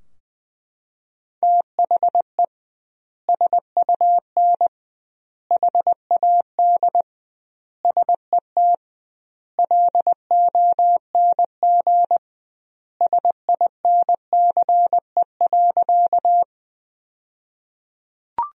Morse code, heard as a keyed tone, about 20 wpm, 700 Hz. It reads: THE SUN HAD SET LONG SINCE.
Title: 2x Word-Spacing - Sentences from Top 300 Words - 20wpm